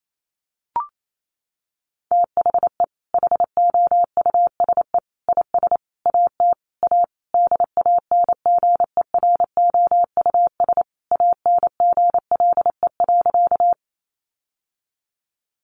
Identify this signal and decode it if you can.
Morse code, heard as a keyed tone, about 28 wpm, 700 Hz. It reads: THE HOUSE IS AT A DANGEROUS ANGLE.
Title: Sentences from Top 1500 Words 28wpm